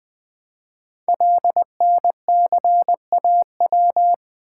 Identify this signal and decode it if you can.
Morse code, heard as a keyed tone, about 20 wpm, 700 Hz. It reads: LNCAW
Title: Code Groups - 5 Letters - 20wpm